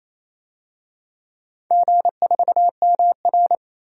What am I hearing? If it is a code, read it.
Morse code, heard as a keyed tone, about 27 wpm, 700 Hz. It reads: G4MR
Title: DXCC-100 Call Signs - 4 Characters - 28wpm